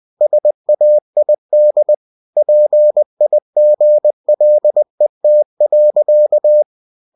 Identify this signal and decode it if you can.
Morse code, heard as a keyed tone, about 20 wpm, 600 Hz. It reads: SAID PIGLET.
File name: winnie-20wpm